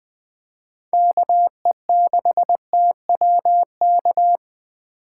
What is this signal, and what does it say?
Morse code, heard as a keyed tone, about 20 wpm, 700 Hz. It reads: KE6TW K
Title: USA Call Signs - 8x Character Spacing - 20wpm